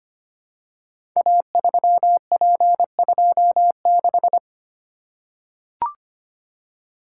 Morse code, heard as a keyed tone, about 25 wpm, 700 Hz. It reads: A3P26
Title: Code Groups - 5 Letter:Numbers - 25wpm